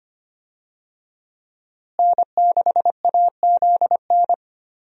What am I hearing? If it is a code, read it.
Morse code, heard as a keyed tone, about 25 wpm, 700 Hz. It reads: N6AZN
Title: USA Call Signs - 5 Characters - Repeated 2x - 25wpm